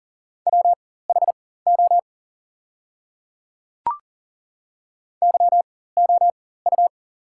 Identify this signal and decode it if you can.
Morse code, heard as a keyed tone, about 37 wpm, spaced as for 18 wpm, 700 Hz. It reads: WHO YOU
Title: 4x Character Spacing - Top 100 Words - 40wpm